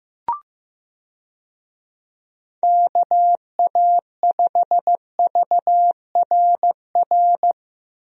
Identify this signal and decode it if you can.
Morse code, heard as a keyed tone, about 15 wpm, 700 Hz. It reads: KA5VRR
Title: USA Call Signs - 6 Characters - 15wpm